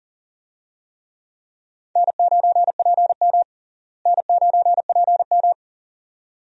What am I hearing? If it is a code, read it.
Morse code, heard as a keyed tone, about 40 wpm, 700 Hz. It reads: N9PM N9PM
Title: USA Call Signs - 4 Characters - Repeated 2x - 40wpm